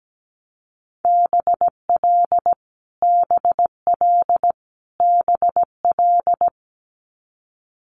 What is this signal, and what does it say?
Morse code, heard as a keyed tone, about 17 wpm, 700 Hz. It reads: BL BL BL
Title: Multi-letter Phonemes Repeated 3x 17wpm